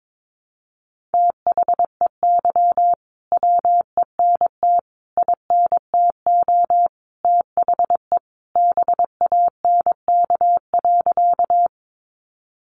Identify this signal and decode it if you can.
Morse code, heard as a keyed tone, about 22 wpm, 700 Hz. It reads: THEY WENT INTO THE BANK.
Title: Sentences from Top 900 Words 22wpm